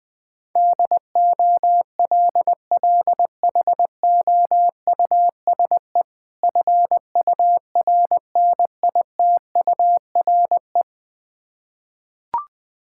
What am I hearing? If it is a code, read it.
Morse code, heard as a keyed tone, about 20 wpm, 700 Hz. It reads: DOLLHOUSE FURNITURE
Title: Difficult Words with Hints - 20wpm